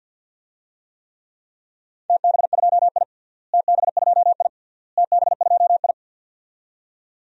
Morse code, heard as a keyed tone, about 48 wpm, 700 Hz. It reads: TB2I TB2I TB2I
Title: DXCC-100 Call Signs - 4 Characters - Repeated 3x - 50wpm